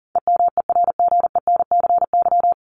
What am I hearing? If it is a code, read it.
Morse code, heard as a keyed tone, about 40 wpm, 700 Hz. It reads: EMERGENCY
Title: Top 1501-1750 Words 40wpm